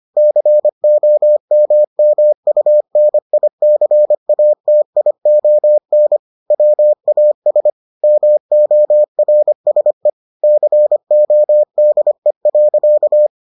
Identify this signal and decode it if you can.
Morse code, heard as a keyed tone, about 25 wpm, 600 Hz. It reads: COMMUNICATION WAS MORSE CODE.